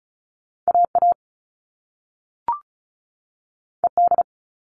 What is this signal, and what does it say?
Morse code, heard as a keyed tone, about 36 wpm, 700 Hz. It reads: AA ED
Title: All Cut Numbers - 2-Digit 35wpm